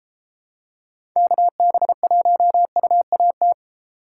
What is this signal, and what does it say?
Morse code, heard as a keyed tone, about 32 wpm, 700 Hz. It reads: KB1UAT